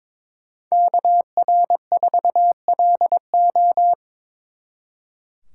Morse code, heard as a keyed tone, about 26 wpm, 700 Hz. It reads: KR4LO
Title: Speed Racing - USA 2x2 Call Signs 22wpm